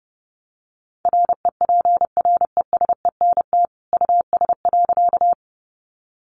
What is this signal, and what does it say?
Morse code, heard as a keyed tone, about 30 wpm, 700 Hz. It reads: REPRESENT US.
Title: Sentences from Top 1200 Words 30wpm